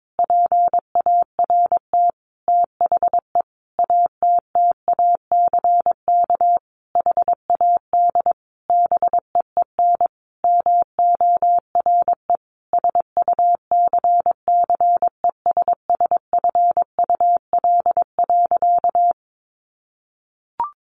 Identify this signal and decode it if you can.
Morse code, heard as a keyed tone, about 22 wpm, 700 Hz. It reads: PART THE ATTACK HAD BEEN MORE SUCCESSFUL.
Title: Sentences from Top 1500 Words 22wpm